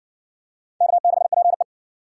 Morse code, heard as a keyed tone, about 60 wpm, 700 Hz. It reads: K6PE